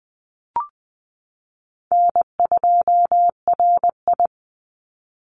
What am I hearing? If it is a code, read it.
Morse code, heard as a keyed tone, about 20 wpm, 700 Hz. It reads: N2RI
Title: Call Signs Repeated 2x - 20wpm